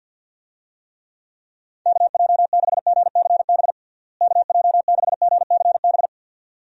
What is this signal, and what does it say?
Morse code, heard as a keyed tone, about 50 wpm, 700 Hz. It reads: KJ6GCB KJ6GCB
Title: USA Call Signs - 6 Characters - Repeated 2x - 50wpm